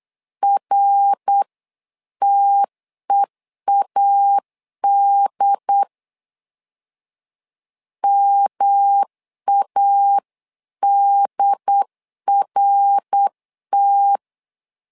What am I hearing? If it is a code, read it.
Morse code, heard as a keyed tone, about 8 wpm, 800 Hz. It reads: RTEAD MADRT
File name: LEZIONI / 3 MWRO / 3 MWR 8wpm 2444